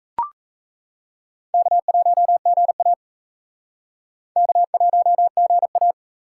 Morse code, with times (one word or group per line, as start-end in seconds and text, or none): 1.5-2.9 K1GA
4.4-5.9 K1GA